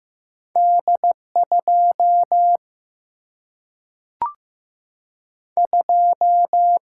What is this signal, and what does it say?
Morse code, heard as a keyed tone, about 15 wpm, 700 Hz. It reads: D 2 2
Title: Single-Letter Single-Number 15wpm